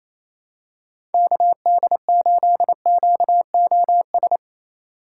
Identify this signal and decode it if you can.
Morse code, heard as a keyed tone, about 28 wpm, 700 Hz. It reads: KD8QOS
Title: USA Call Signs - 6 Characters - 3x Character Spacing - 28wpm